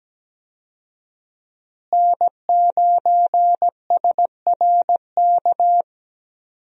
Morse code, heard as a keyed tone, about 17 wpm, 700 Hz. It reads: N9SRK
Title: USA Call Signs - 5 Characters - Repeated 2x - 17wpm